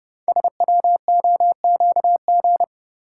Morse code, heard as a keyed tone, about 30 wpm, 700 Hz. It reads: SWOQG
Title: Code Groups - 5 Letters - 30wpm